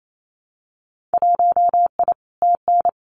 Morse code, heard as a keyed tone, about 28 wpm, 700 Hz. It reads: 1I TN
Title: Winter Field Day Exchanges Encore 28wpm